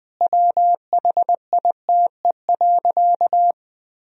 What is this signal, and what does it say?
Morse code, heard as a keyed tone, about 20 wpm, 700 Hz. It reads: WHITE.
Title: Sentences from Top 1000 Words - Review Part 1 - 20wpm